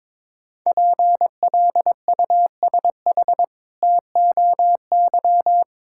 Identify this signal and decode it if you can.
Morse code, heard as a keyed tone, about 22 wpm, 700 Hz. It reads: PLUSH TOY
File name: Difficult Words with Hints - 22wpm